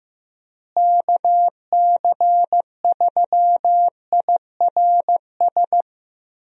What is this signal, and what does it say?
Morse code, heard as a keyed tone, about 15 wpm, 700 Hz. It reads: KC3IRS
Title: USA Call Signs - 6 Characters - Repeated 3x - 15wpm